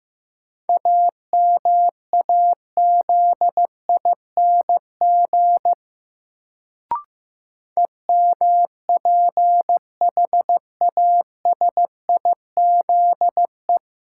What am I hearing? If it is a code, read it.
Morse code, heard as a keyed tone, about 15 wpm, 700 Hz. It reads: AMAZING EMPHASIZE
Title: Sentences from Top 1500 Words - New Word Drill Alt - 15wpm